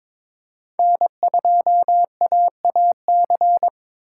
Morse code, heard as a keyed tone, about 22 wpm, 700 Hz. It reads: N2AAC
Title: USA Call Signs - 4x Character Spacing - 22wpm